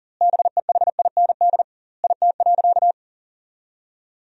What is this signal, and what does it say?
Morse code, heard as a keyed tone, about 40 wpm, 700 Hz. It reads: BEHIND IT.
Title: 2x Word-Spacing - Sentences from Top 500 Words - 40wpm